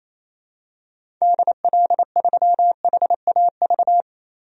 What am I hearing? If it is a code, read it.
Morse code, heard as a keyed tone, about 27 wpm, 700 Hz. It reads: DL3HAV
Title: DXCC-100 Call Signs - 6 Characters - Repeated 2x - 28wpm